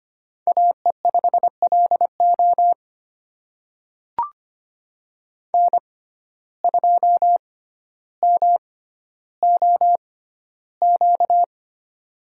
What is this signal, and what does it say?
Morse code, heard as a keyed tone, about 25 wpm, spaced as for 13 wpm, 700 Hz. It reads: AE5LO N2MOQ
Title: USA Call Signs - 5 Characters - 6x Character Spacing - 25wpm